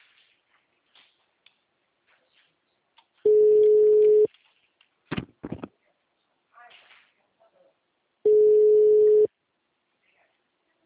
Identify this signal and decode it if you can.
Ringback tone (Europe, 425 Hz), with speech.
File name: out- -2002- - - .6